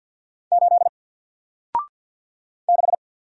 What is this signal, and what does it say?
Morse code, heard as a keyed tone, about 50 wpm, 700 Hz. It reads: 8 6
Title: Single Character (Letter:Number:Punctuation:Prosign) Alt 50wpm